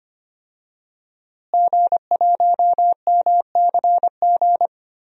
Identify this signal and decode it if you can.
Morse code, heard as a keyed tone, about 25 wpm, 700 Hz. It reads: G1MCG
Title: DXCC-100 Call Signs - 5 Characters - Repeated 3x - 25wpm